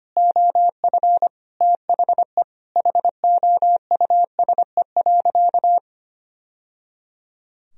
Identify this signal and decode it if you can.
Morse code, heard as a keyed tone, about 25 wpm, 700 Hz. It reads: OF THE HOUSE.
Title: Sentences from Top 1000 Words - Review Part 2 - 25wpm